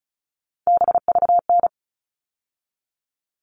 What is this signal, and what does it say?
Morse code, heard as a keyed tone, about 34 wpm, 700 Hz. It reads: BVN
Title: All Cut Numbers - 3-Digit 35wpm